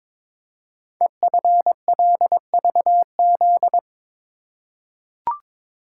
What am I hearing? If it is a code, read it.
Morse code, heard as a keyed tone, about 22 wpm, 700 Hz. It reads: EFLVZ
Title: Code Groups - 5 Letters - 22wpm